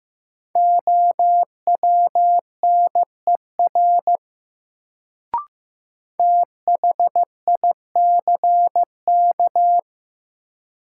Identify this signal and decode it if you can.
Morse code, heard as a keyed tone, about 15 wpm, 700 Hz. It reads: OWNER THICK
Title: Sentences from Top 1600 Words - New Word Drill Alt - 15wpm